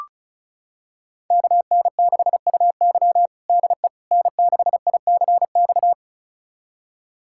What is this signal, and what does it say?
Morse code, heard as a keyed tone, about 36 wpm, 700 Hz. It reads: KN6UY DE N6ICX
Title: USA Callsign DE Callsign - 5 Characters - 35wpm